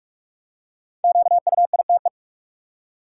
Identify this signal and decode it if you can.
Morse code, heard as a keyed tone, about 41 wpm, 700 Hz. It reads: QUITE
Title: Five Letter Words - 45wpm